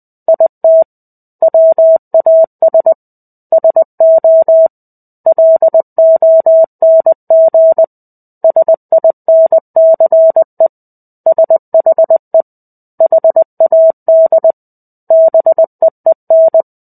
Morse code, heard as a keyed tone, about 20 wpm, 650 Hz. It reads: IT WAS SO LONG SINCE SHE HAD BEEN
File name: 05-Chapter-05 20 WPM